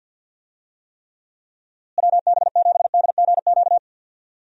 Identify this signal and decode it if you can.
Morse code, heard as a keyed tone, about 52 wpm, 700 Hz. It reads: WB7DGQ
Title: USA Call Signs - 6 Characters - Repeated 2x - 50wpm